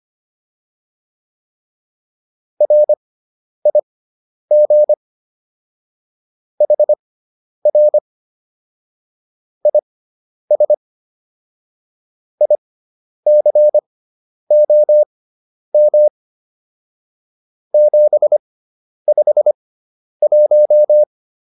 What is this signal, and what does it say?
Morse code, heard as a keyed tone, about 25 wpm, spaced as for 10 wpm, 600 Hz. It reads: RIG HR IS ICOM 751
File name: QSO210 10